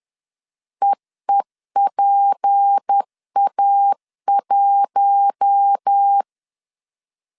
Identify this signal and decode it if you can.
Morse code, heard as a keyed tone, about 10 wpm, 800 Hz. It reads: EEPA1
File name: CW / DET / 1 6 1500C S A 10wpm 4141